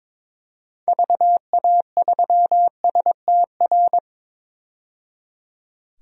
Morse code, heard as a keyed tone, about 22 wpm, 700 Hz. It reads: VA3STR
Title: DXCC-100 Call Signs - 6 Characters - 22wpm